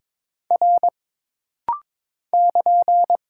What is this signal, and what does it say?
Morse code, heard as a keyed tone, about 23 wpm, 700 Hz. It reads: R <KN>
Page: This rapid-fire timing is geared to push your ICR (Instant Character Recognition) skills.